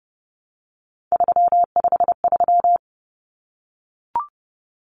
Morse code, heard as a keyed tone, about 30 wpm, 700 Hz. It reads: 353
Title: Numbers - 3-Digit 30wpm